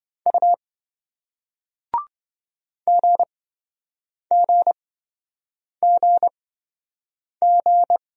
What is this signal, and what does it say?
Morse code, heard as a keyed tone, about 26 wpm, spaced as for 7 wpm, 700 Hz. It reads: U GGGG